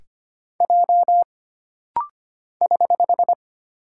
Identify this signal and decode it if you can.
Morse code, heard as a keyed tone, about 25 wpm, 700 Hz. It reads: J <HH>
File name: Single Character (Letter:Number:Punctuation:Prosign) - Rapid-Fire - 25wpm